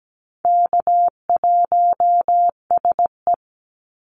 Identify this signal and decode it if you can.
Morse code, heard as a keyed tone, about 17 wpm, 700 Hz. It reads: K1SE
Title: Easy Call Signs Repeated 2x - 17wpm